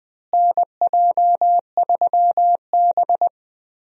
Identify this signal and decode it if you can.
Morse code, heard as a keyed tone, about 20 wpm, 700 Hz. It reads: NJ3B
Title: USA Call Signs - 4 Characters - Repeated 3x - 20wpm